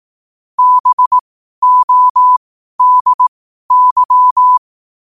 Morse code, heard as a keyed tone, about 18 wpm, spaced as for 12 wpm, 1000 Hz. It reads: BODY
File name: qotdfiles / 12wpm